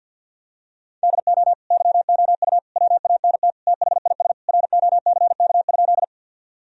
Morse code, heard as a keyed tone, about 52 wpm, 700 Hz. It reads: DO YOU WANT THIS ROCK?
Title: Sentences from Top 1000 Words - Review Part 2 - 50wpm